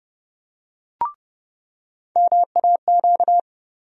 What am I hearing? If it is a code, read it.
Morse code, heard as a keyed tone, about 30 wpm, 700 Hz. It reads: MAQ
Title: USA Call Sign Postfixes - 3 Letters 30wpm